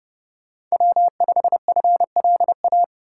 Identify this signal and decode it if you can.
Morse code, heard as a keyed tone, about 36 wpm, 700 Hz. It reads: W5FLA
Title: Speed Racing - USA 1x3 Call Signs 30wpm